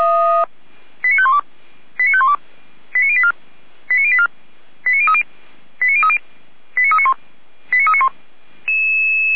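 Five-tone selective call (ZVEI), heard as quick runs of five tones, tones of about 70 ms. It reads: 89512 89512 89084 89084 89030 89030 89391 89391
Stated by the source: ZVEI five-tone selective call received on an SDR by feuerwehr-orlamuende.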